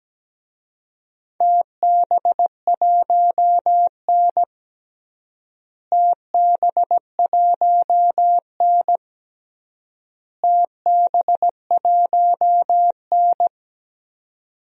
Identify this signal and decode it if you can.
Morse code, heard as a keyed tone, about 17 wpm, 700 Hz. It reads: TB1N TB1N TB1N
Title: DXCC-100 Call Signs - 4 Characters - Repeated 3x - 17wpm